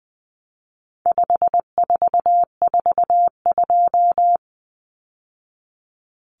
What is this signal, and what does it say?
Morse code, heard as a keyed tone, about 20 wpm, 700 Hz. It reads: 5442
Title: Numbers - 4-Digit 20wpm